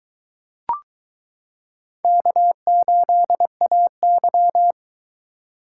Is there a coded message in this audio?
Yes: Morse code from 2.0 to 4.7 s, heard as a keyed tone, about 22 wpm, 700 Hz. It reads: K8AY